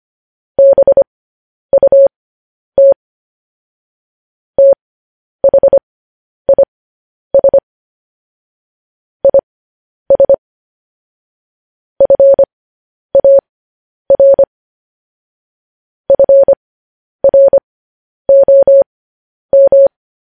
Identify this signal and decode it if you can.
Morse code, heard as a keyed tone, about 25 wpm, spaced as for 10 wpm, 550 Hz. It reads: BUT THIS IS FAR FROM
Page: news headlines in morse code at 10 wpm